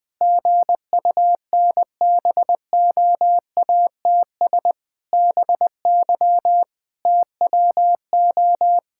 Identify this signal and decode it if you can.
Morse code, heard as a keyed tone, about 20 wpm, 700 Hz. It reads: GUNBOATS BY TWO
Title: Pangram Challenge v2 - 20wpm